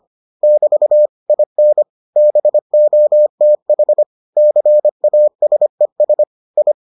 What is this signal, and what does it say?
Morse code, heard as a keyed tone, about 25 wpm, 600 Hz. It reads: <BT> IN BOTH CASES I